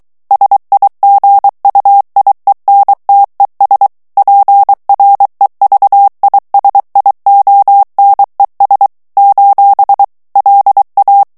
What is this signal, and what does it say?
Morse code, heard as a keyed tone, about 23 wpm, 800 Hz. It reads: SIGUIENTES PREVISIONES : LA
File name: morse mp3 / ESPAGNOL TEXTE EN CLAIR 2 VITESSE 22 MOTS MINUTES